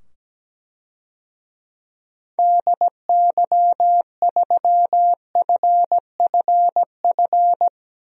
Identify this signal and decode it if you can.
Morse code, heard as a keyed tone, about 17 wpm, 700 Hz. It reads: DY3FFF
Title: DXCC-100 Call Signs - 6 Characters - 17wpm